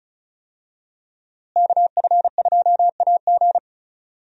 Morse code, heard as a keyed tone, about 36 wpm, 700 Hz. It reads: KF2AG
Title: USA Call Signs - 5 Characters - Repeated 3x - 35wpm